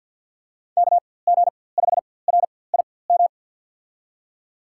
Morse code, heard as a keyed tone, about 47 wpm, spaced as for 23 wpm, 700 Hz. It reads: KG5RIM
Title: USA Call Signs - 6 Characters - 4x Character Spacing - 50wpm